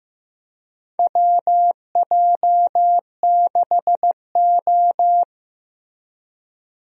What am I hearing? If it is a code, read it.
Morse code, heard as a keyed tone, about 15 wpm, 700 Hz. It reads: WJ6O
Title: USA Call Signs - 4 Characters - Repeated 2x - 15wpm